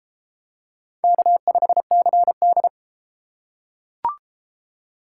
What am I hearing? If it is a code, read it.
Morse code, heard as a keyed tone, about 32 wpm, 700 Hz. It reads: K5CD